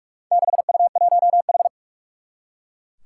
Morse code, heard as a keyed tone, about 44 wpm, 700 Hz. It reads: 6U1H
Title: Four Challenging Letter-Number Combinations 45wpm